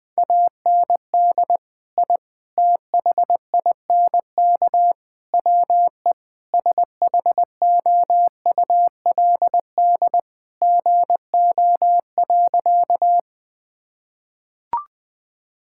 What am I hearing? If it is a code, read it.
Morse code, heard as a keyed tone, about 21 wpm, 700 Hz. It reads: AND I THINK WE SHOULD GO.
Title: Sentences with 4-Word Phrases - 20wpm